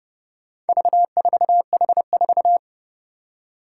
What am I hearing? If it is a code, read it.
Morse code, heard as a keyed tone, about 32 wpm, 700 Hz. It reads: V4H4
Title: Four Challenging Letter-Number Combinations 30wpm